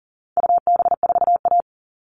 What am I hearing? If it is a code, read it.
Morse code, heard as a keyed tone, about 40 wpm, 700 Hz. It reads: UB4A
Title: All Cut Numbers - 4-Digit 40wpm